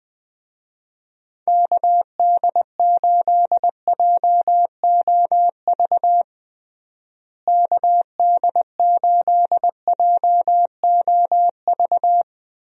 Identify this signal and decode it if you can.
Morse code, heard as a keyed tone, about 21 wpm, 700 Hz. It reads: KD8JOV KD8JOV
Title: USA Call Signs - 6 Characters - Repeated 3x - 20wpm